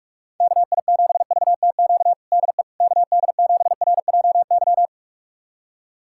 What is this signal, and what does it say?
Morse code, heard as a keyed tone, about 46 wpm, 700 Hz. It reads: KI7VTQ DE KD7RJY
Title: USA Callsign DE Callsign - 6 Characters - 45wpm